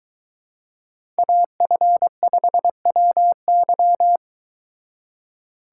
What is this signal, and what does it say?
Morse code, heard as a keyed tone, about 22 wpm, 700 Hz. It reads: AF5WY